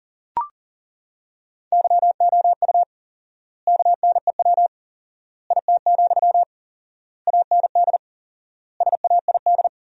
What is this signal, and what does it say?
Morse code, heard as a keyed tone, about 37 wpm, 700 Hz. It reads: YOU KNEW IT, AND SAID
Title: 4x Word-Spacing - Sentences from Top 500 Words - 40wpm